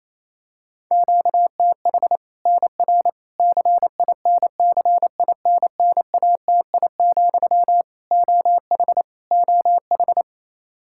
Morse code, heard as a keyed tone, about 28 wpm, 700 Hz. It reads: QTH NR CINCINNATI, OH OH
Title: Practice QSO - Exchange 1 - Beginning - 28wpm